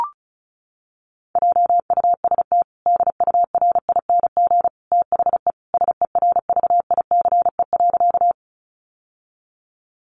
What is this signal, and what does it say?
Morse code, heard as a keyed tone, about 34 wpm, 700 Hz. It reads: JUST DURING THE SERVICE.
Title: Sentences from Top 700 Words 35wpm